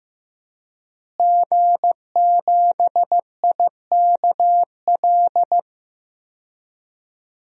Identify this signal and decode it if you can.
Morse code, heard as a keyed tone, about 15 wpm, 700 Hz. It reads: G7IKL
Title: DXCC-100 Call Signs - 5 Characters - Repeated 2x - 15wpm